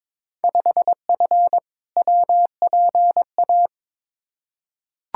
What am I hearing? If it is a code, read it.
Morse code, heard as a keyed tone, about 22 wpm, 700 Hz. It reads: 5F WPA
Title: Field Day Exchanges Encore - 22wpm